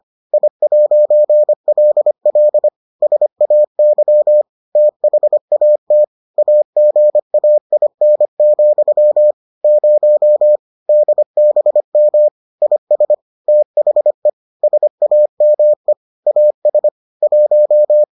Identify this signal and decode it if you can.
Morse code, heard as a keyed tone, about 25 wpm, 600 Hz. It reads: I'LL SAY THAT AGAIN, 0 DBM IS THE SAME AS 1